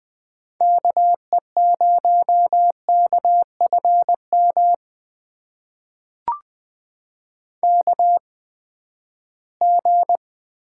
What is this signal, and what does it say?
Morse code, heard as a keyed tone, about 20 wpm, spaced as for 14 wpm, 700 Hz. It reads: KE0KFM KG
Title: USA Call Signs - 6 Characters - 8x Character Spacing - 20wpm